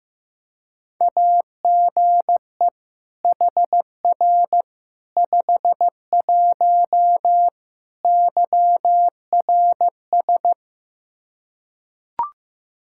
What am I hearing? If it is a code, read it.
Morse code, heard as a keyed tone, about 15 wpm, 700 Hz. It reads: AGE HR 51 YRS
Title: Practice QSO - Exchange 3 - More Info - 15wpm